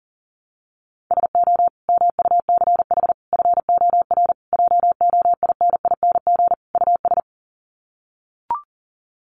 Morse code, heard as a keyed tone, about 40 wpm, 700 Hz. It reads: SO MUCH FOR JOINING US
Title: Sets of 5 Words 40wpm